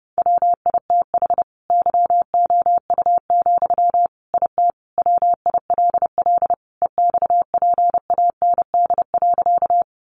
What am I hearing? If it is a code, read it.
Morse code, heard as a keyed tone, about 30 wpm, 700 Hz. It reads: WITH YOU, IT WILL EXPAND.
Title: Sentences from Top 1400 Words 30wpm